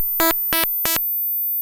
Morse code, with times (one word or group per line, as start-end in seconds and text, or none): none